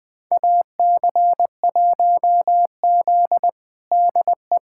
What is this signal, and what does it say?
Morse code, heard as a keyed tone, about 21 wpm, 700 Hz. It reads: AC1Z DE
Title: USA Callsign DE Callsign 20wpm